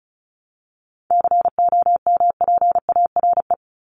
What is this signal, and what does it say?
Morse code, heard as a keyed tone, about 34 wpm, 700 Hz. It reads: COMPARE
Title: Top 1251-1500 Words 35wpm